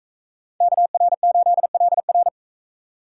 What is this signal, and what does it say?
Morse code, heard as a keyed tone, about 39 wpm, 700 Hz. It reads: KR8LR